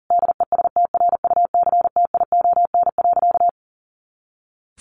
Morse code, heard as a keyed tone, about 40 wpm, 700 Hz. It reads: DESTRUCTION.
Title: A Sound Of Thunder 40wpm